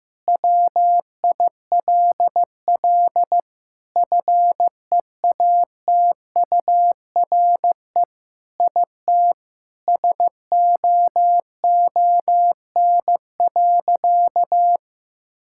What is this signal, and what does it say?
Morse code, heard as a keyed tone, about 15 wpm, 700 Hz. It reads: WILL FEATURE IT SOON.
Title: Sentences from Top 1000 Words - Review Part 5 - 15wpm